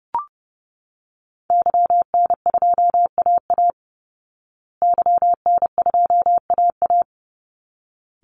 Morse code, heard as a keyed tone, about 30 wpm, 700 Hz. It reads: YN2AA YN2AA
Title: Call Signs Repeated 3x - 30wpm